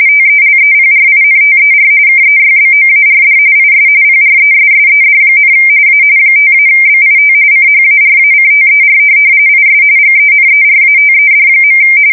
RTTY, about 45.45 baud. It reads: TELEPRINTER KEYBOARD IS PRESSED, A 5-BIT CHARACTER IS GENERATED.
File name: fsk170 / teletype-RTTY